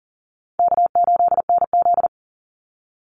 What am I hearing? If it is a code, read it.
Morse code, heard as a keyed tone, about 40 wpm, 700 Hz. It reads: K8NZ
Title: Easy Call Signs Repeated 3x - 40wpm